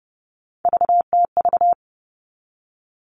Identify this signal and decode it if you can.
Morse code, heard as a keyed tone, about 30 wpm, 700 Hz. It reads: VTV
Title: All Cut Numbers - 3-Digit 30wpm